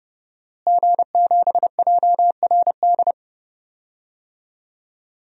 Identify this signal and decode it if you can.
Morse code, heard as a keyed tone, about 30 wpm, 700 Hz. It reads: G7JRD
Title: DXCC-100 Call Signs - 5 Characters - Repeated 3x - 30wpm